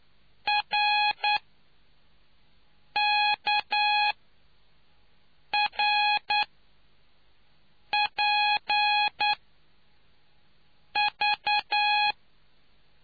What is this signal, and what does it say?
Morse code, heard as a keyed tone, about 9 wpm, spaced as for 4 wpm, 830 Hz. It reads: RKRPV